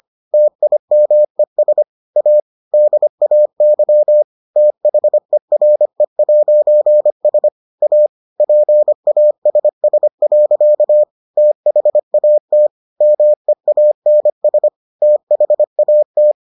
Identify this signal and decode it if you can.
Morse code, heard as a keyed tone, about 25 wpm, 600 Hz. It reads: TIMES A DAY THERE'S A PASS. THAT MEANS THAT